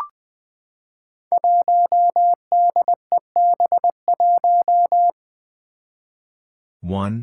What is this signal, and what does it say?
Morse code, heard as a keyed tone, about 20 wpm, 700 Hz. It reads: 1DEB1
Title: Five Challenging Letter-Number Combinations 20wpm